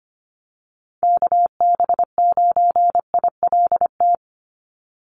Morse code, heard as a keyed tone, about 25 wpm, 700 Hz. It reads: KB9ILT
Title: NA SOTA Activator Callsigns 25wpm